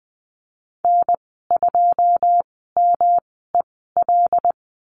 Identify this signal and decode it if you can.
Morse code, heard as a keyed tone, about 20 wpm, spaced as for 15 wpm, 700 Hz. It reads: N2MEL
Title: 167 Lesson 27g - Number 7 - Call Signs 20wpm